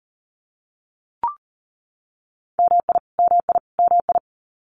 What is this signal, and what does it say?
Morse code, heard as a keyed tone, about 40 wpm, 700 Hz. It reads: MI MI MI
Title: US States Repeated 3x 40wpm